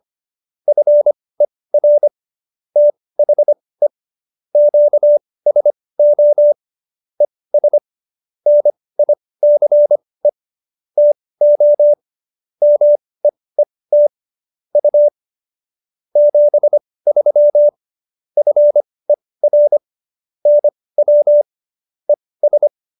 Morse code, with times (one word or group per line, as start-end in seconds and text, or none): 0.7-2.1 FER
2.8-3.9 THE
4.5-6.5 QSO
7.2-7.8 ES
8.5-10.3 NICE
11.0-11.9 TO
12.6-14.1 MEET
14.8-15.1 U
16.2-17.7 73
18.4-19.8 FER
20.5-21.4 NW
22.1-22.7 ES